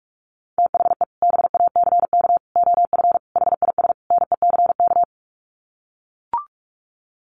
Morse code, heard as a keyed tone, about 44 wpm, 700 Hz. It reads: THE BACK OF HIS NECK
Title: Sets of 5 Words Encore 45wpm